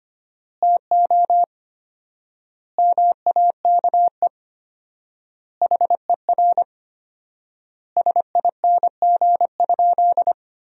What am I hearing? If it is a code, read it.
Morse code, heard as a keyed tone, about 25 wpm, 700 Hz. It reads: TO MAKE HER SING?
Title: 4x Word-Spacing - Sentences from Top 400 Words - 25wpm